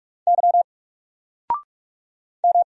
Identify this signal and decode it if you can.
Morse code, heard as a keyed tone, about 44 wpm, 700 Hz. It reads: Y M
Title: Single Character (Letter:Number:Punctuation:Prosign) 45wpm